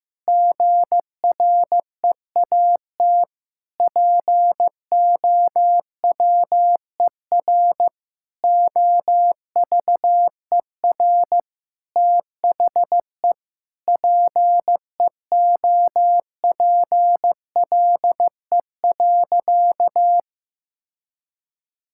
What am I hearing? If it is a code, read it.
Morse code, heard as a keyed tone, about 15 wpm, 700 Hz. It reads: GREAT POWER OVER THE PEOPLE.
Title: Sentences from Top 1000 Words - Review Part 5 - 15wpm